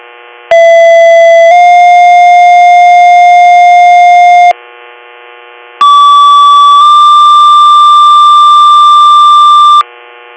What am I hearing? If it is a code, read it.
Two-tone page, heard as a one-second tone, then a three-second tone, about 670 then 710 Hz, about 1150 then 1180 Hz. It is heard from 0.5 to 9.8 s.